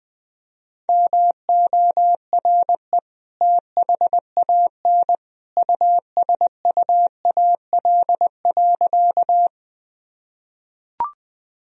Morse code, heard as a keyed tone, about 20 wpm, 700 Hz. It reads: MORE THAN USUAL.
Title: Sentences from Top 1000 Words - Review Part 2 - 20wpm